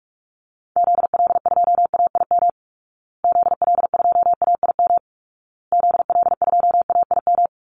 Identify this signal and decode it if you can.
Morse code, heard as a keyed tone, about 44 wpm, 700 Hz. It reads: ZL2AIM ZL2AIM ZL2AIM
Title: Call Signs Repeated 2x - 45wpm